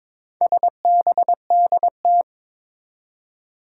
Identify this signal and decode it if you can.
Morse code, heard as a keyed tone, about 23 wpm, 700 Hz. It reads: SBDT
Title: Four Challenging Letter-Number Combinations 22wpm